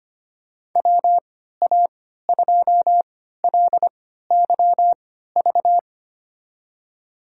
Morse code, heard as a keyed tone, about 25 wpm, spaced as for 14 wpm, 700 Hz. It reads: WA2LYV
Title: USA Call Signs - 6 Characters - 3x Character Spacing - 25wpm